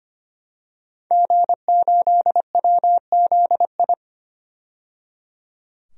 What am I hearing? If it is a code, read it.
Morse code, heard as a keyed tone, about 25 wpm, 700 Hz. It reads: G8WZI
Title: DXCC-100 Call Signs - 5 Characters - 25wpm